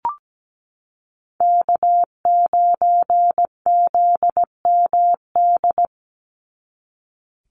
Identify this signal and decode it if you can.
Morse code, heard as a keyed tone, about 17 wpm, 700 Hz. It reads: K9ZMD
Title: NA SOTA Chaser Callsigns 17wpm